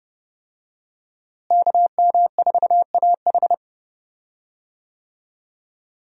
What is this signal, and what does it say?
Morse code, heard as a keyed tone, about 30 wpm, 700 Hz. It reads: KM4AH
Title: USA Call Signs - 5 Characters - Repeated 2x - 30wpm